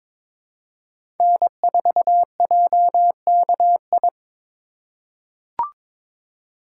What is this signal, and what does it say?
Morse code, heard as a keyed tone, about 22 wpm, 700 Hz. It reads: N4JKI
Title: USA Call Signs - 5 Characters - 4x Character Spacing - 22wpm